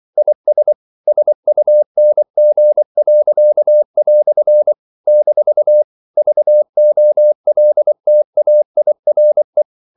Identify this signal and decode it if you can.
Morse code, heard as a keyed tone, about 24 wpm, 600 Hz. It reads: IS SUNG." - VOLTAIRE
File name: CW / Practice / EnglishQuotations0324